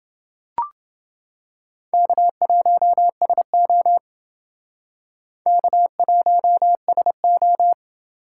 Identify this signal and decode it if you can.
Morse code, heard as a keyed tone, about 29 wpm, 700 Hz. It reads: K1SO K1SO